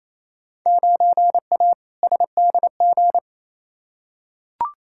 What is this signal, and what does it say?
Morse code, heard as a keyed tone, about 28 wpm, 700 Hz. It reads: 9A SDG
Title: Field Day Exchanges Encore - 28wpm